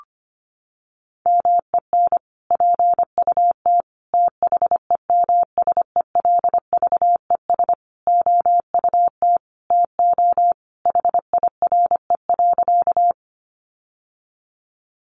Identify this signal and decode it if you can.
Morse code, heard as a keyed tone, about 25 wpm, 700 Hz. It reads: MEN PUT THEMSELVES OUT TO HIRE.
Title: Sentences from Top 1400 Words 25wpm